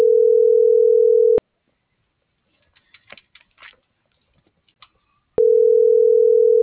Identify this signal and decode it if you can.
Ringback tone (North America), 440 + 480 Hz.